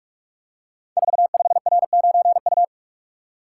Morse code, heard as a keyed tone, about 47 wpm, 700 Hz. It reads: 45R9U
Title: Code Groups - 5 Letter:Numbers - 45wpm